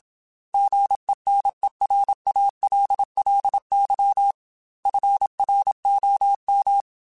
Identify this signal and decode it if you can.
Morse code, heard as a keyed tone, about 27 wpm, 800 Hz. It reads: GENERALLY FROM
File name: Weather Sets / Weather Sets-mp3s / Weather 5 12 @25